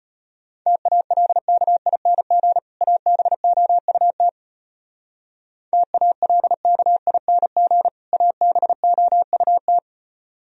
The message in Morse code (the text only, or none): TALKING ABOUT TALKING ABOUT